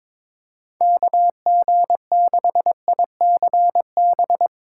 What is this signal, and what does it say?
Morse code, heard as a keyed tone, about 22 wpm, 700 Hz. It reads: KG6ICB
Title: USA Call Signs - 6 Characters - 4x Character Spacing - 22wpm